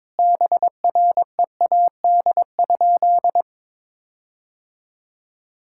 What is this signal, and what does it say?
Morse code, heard as a keyed tone, about 22 wpm, 700 Hz. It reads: BREAD?
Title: Sentences from Top 1600 Words 22wpm